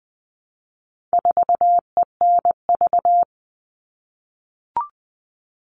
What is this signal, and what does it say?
Morse code, heard as a keyed tone, about 20 wpm, 700 Hz. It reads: 4ENV
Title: All Cut Numbers - 4-Digit 20wpm